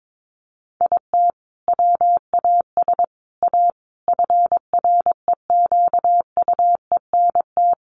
Morse code, heard as a keyed tone, about 22 wpm, 700 Hz. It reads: IT WAS A FREQUENT
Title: Sentences from Top 1400 Words 22wpm